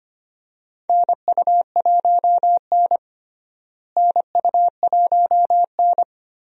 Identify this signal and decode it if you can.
Morse code, heard as a keyed tone, about 25 wpm, 700 Hz. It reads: NU1N NU1N
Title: USA Call Signs - 4 Characters - Repeated 3x - 25wpm